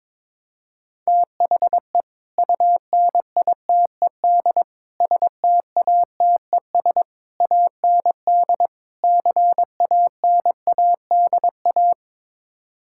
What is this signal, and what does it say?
Morse code, heard as a keyed tone, about 22 wpm, 700 Hz. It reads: THE UNITED STATES AND CANADA
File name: practice / 22wpm / Build A Phrase - 5-Word Phrases - 22wpm